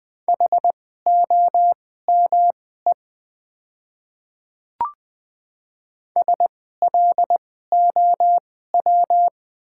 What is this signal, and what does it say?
Morse code, heard as a keyed tone, about 20 wpm, spaced as for 14 wpm, 690 Hz. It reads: HOME SLOW
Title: 2x Character Spacing - Top 500 Words - 20wpm